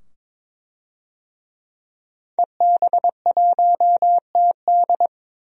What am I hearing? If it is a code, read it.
Morse code, heard as a keyed tone, about 22 wpm, 700 Hz. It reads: EB1TD
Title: DXCC-100 Call Signs - 5 Characters - 22wpm